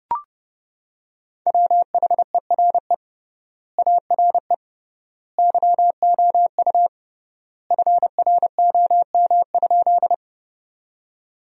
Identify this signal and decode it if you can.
Morse code, heard as a keyed tone, about 30 wpm, 700 Hz. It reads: WHERE ARE YOU FROM?
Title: 3x Word-Spacing - Sentences from Top 200 Words - 30wpm